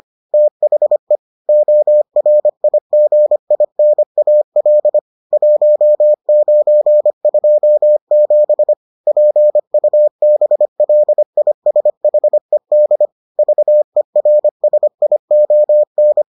Morse code, heard as a keyed tone, about 25 wpm, 600 Hz. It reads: THE ORIGINAL 1927 PUBLISHED VERSION